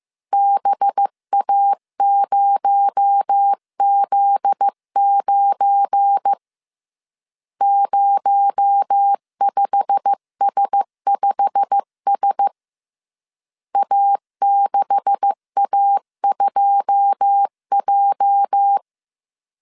Morse code, heard as a keyed tone, about 15 wpm, 800 Hz. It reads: BA0Z9 05S5S A6A2J
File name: CW / DET / 1 9 1500C S B 14wpm 3302